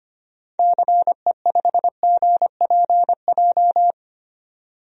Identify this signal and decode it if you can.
Morse code, heard as a keyed tone, about 25 wpm, 700 Hz. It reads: CE5GPJ
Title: DXCC-100 Call Signs - 6 Characters - 25wpm